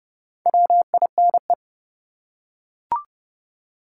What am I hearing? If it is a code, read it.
Morse code, heard as a keyed tone, about 31 wpm, 700 Hz. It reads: WINE
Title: Four Letter Words - 30wpm